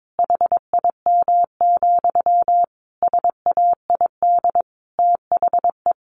Morse code, heard as a keyed tone, about 22 wpm, 700 Hz. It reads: HIM, SAID THE
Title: Sentences from Top 400 Words 22wpm